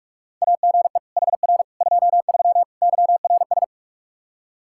Morse code, heard as a keyed tone, about 46 wpm, 700 Hz. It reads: AGE HR 23 YRS
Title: Practice QSO - Exchange 3 - More Info - 45wpm